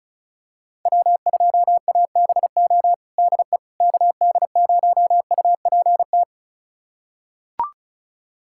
Morse code, heard as a keyed tone, about 36 wpm, 700 Hz. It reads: W2ABO DE KD0UPT
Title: USA Callsign DE Callsign 35wpm